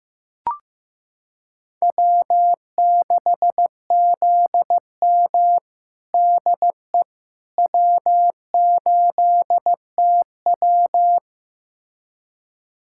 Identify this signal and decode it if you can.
Morse code, heard as a keyed tone, about 15 wpm, 700 Hz. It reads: W6ZM DE W8TW
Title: USA Callsign DE Callsign - 4 Characters - 15wpm